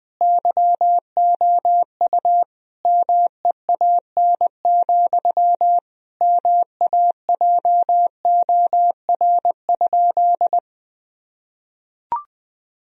Morse code, heard as a keyed tone, about 20 wpm, 700 Hz. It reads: YOU MEAN, MAJOR?
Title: Sentences from Top 1000 Words - Review Part 4 - 20wpm